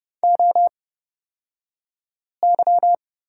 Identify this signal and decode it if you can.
Morse code, heard as a keyed tone, about 30 wpm, spaced as for 5 wpm, 690 Hz. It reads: OY
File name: Lesson 36d - BT - Two Characters 30wpm